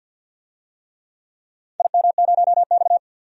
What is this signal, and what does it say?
Morse code, heard as a keyed tone, about 52 wpm, 700 Hz. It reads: IM0X
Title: DXCC-100 Call Signs - 4 Characters - 50wpm